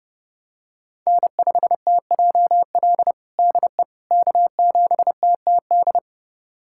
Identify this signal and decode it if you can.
Morse code, heard as a keyed tone, about 30 wpm, 700 Hz. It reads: N5TJL DE K7TTD
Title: USA Callsign DE Callsign - 5 Characters - 30wpm